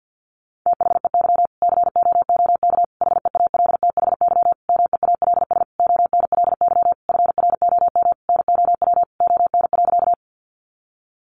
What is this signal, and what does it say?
Morse code, heard as a keyed tone, about 50 wpm, 700 Hz. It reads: THEY COOK HEALTHY MEALS ONLY FROM NOW ON.
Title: Easy Phrases 50wpm